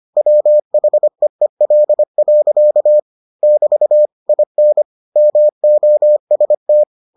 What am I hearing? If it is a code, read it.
Morse code, heard as a keyed tone, about 25 wpm, 600 Hz. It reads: WHEEL. <BT> IN MOST